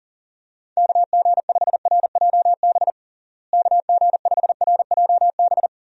Morse code, heard as a keyed tone, about 40 wpm, 700 Hz. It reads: KG5RJB KG5RJB
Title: USA Call Signs - 6 Characters - Repeated 3x - 40wpm